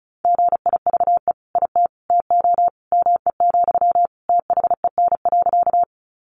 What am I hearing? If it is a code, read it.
Morse code, heard as a keyed tone, about 34 wpm, 700 Hz. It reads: GIVE IT TO ME, THEN.
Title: Sentences from Top 200 Words 35wpm